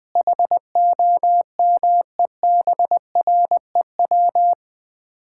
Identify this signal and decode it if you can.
Morse code, heard as a keyed tone, about 21 wpm, 700 Hz. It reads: HOMEBREW
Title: Common QSO Phrases - 3x Word-Spacing - 20wpm